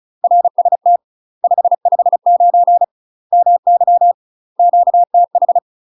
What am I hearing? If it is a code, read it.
Morse code, heard as a keyed tone, about 34 wpm, 700 Hz. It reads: RST 559 MY QTH
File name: UK MASTER QSOs TRAINING / 35WPM / UK MASTER QSOs 0025